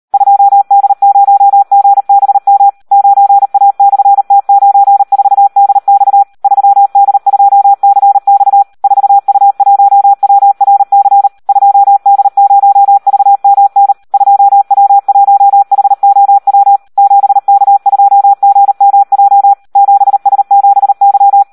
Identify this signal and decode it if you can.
Morse code, heard as a keyed tone, about 38 wpm, 800 Hz. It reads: 2D0GBM 9A/T94DX 3D2CX 4U1WRC 2D0VMN 2W1HOW 7K2GMJ 7S7Y
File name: Audio practica CW / SD 1 38